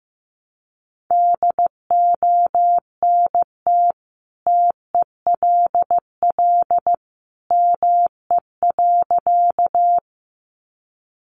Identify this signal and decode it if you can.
Morse code, heard as a keyed tone, about 15 wpm, 700 Hz. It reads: DONT TELL ME.
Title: Sentences from Top 300 Words 15wpm